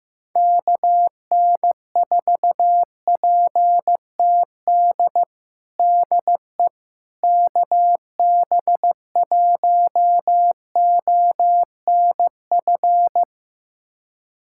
Morse code, heard as a keyed tone, about 15 wpm, 700 Hz. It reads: KN4PTD DE KB1ONF
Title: USA Callsign DE Callsign - 6 Characters - 15wpm